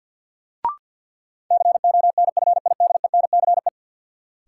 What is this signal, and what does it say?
Morse code, heard as a keyed tone, about 53 wpm, 700 Hz. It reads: CONFIDENCE
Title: Sentences from Top 1500 Words - New Word Drill Alt - 50wpm